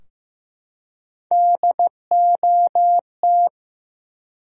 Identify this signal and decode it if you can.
Morse code, heard as a keyed tone, about 15 wpm, 700 Hz. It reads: DOT
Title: USA Common Names - Max 3 Letters - 15wpm